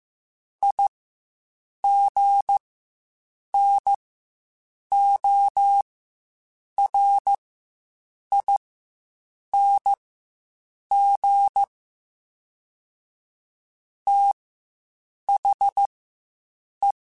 Morse code, heard as a keyed tone, about 15 wpm, spaced as for 7 wpm, 800 Hz. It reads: IGNORING THE
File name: Quotes Spaced / Quotes Spaced-mp3s / IgnoringFacts sp @10